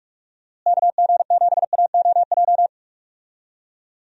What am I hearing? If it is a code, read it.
Morse code, heard as a keyed tone, about 46 wpm, 700 Hz. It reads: KG7AOJ
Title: USA Call Signs - 6 Characters - Repeated 3x - 45wpm